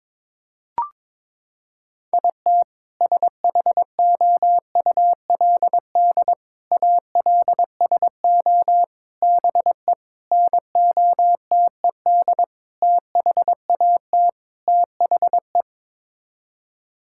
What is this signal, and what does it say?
Morse code, heard as a keyed tone, about 22 wpm, 700 Hz. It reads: IT SHOULD ALSO BE NOTED THAT THE
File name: practice / 22wpm / Sets of 7 Words - 22wpm